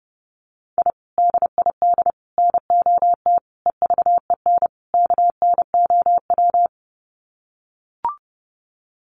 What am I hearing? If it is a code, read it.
Morse code, heard as a keyed tone, about 30 wpm, 700 Hz. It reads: I DID NOT EVEN KNOW
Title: Sets of 5 Words Encore 30wpm